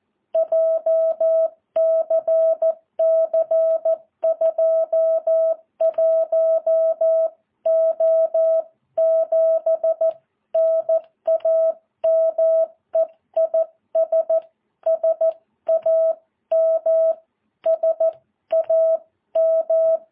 Morse code, heard as a keyed tone, about 14 wpm, 650 Hz. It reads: JCC 21O7 NAME IS SAM SAM
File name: jh8cbh